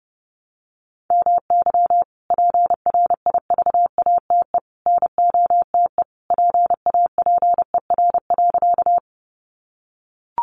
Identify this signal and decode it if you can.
Morse code, heard as a keyed tone, about 30 wpm, 700 Hz. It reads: MY PRIVATE NOTE PAPER.
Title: Sentences from Top 1000 Words 30wpm